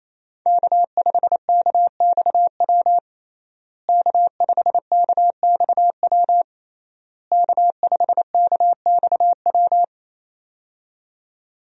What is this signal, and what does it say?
Morse code, heard as a keyed tone, about 28 wpm, 700 Hz. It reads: K5KXW K5KXW K5KXW
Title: USA Call Signs - 5 Characters - Repeated 3x - 28wpm